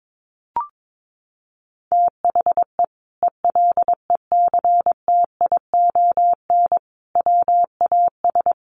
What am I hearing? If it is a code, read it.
Morse code, heard as a keyed tone, about 22 wpm, 700 Hz. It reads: THE ELECTION WAS
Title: Sentences from Top 1500 Words 22wpm